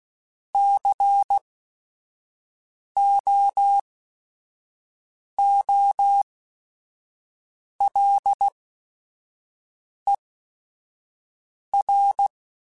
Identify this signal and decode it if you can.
Morse code, heard as a keyed tone, about 16 wpm, spaced as for 5 wpm, 800 Hz. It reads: COOLER
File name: Weather Sets / Weather Sets-mp3s / Weather 5 5 @5